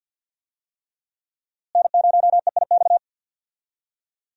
Morse code, heard as a keyed tone, about 51 wpm, 700 Hz. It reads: N0EIX
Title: USA Call Signs - 5 Characters - Repeated 3x - 50wpm